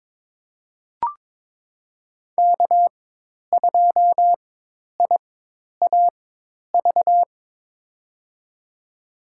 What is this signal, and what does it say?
Morse code, heard as a keyed tone, about 21 wpm, spaced as for 10 wpm, 700 Hz. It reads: K2IAV
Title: USA Call Signs - 5 Characters - 4x Character Spacing - 22wpm